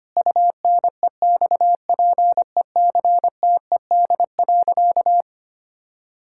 Morse code, heard as a keyed tone, about 25 wpm, 700 Hz. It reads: UNEXPECTED.
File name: practice / 25wpm / Sentences with 2-Word Phrases - 25wpm